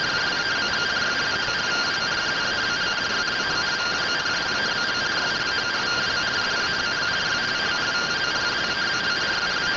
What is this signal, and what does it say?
Two-tone teleprinter signal, about 170 Hz shift.